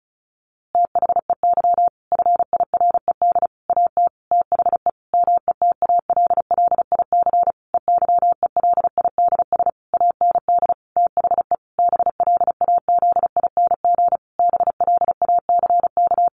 Morse code, heard as a keyed tone, about 34 wpm, 700 Hz. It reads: THEY FIRED AT THE METALLIC EYELIDS AND THE BLAZING BLACK
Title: A Sound Of Thunder 35wpm